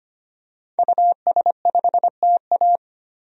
Morse code, heard as a keyed tone, about 25 wpm, 700 Hz. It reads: US5TA
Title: DXCC-100 Call Signs - 5 Characters - 25wpm